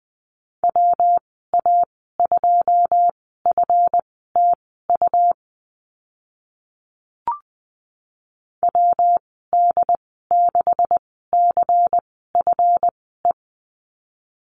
Morse code, heard as a keyed tone, about 20 wpm, spaced as for 14 wpm, 700 Hz. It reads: WA2FTU WD6CFE
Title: 140 Lesson 23g - Letter f - Call Signs 20wpm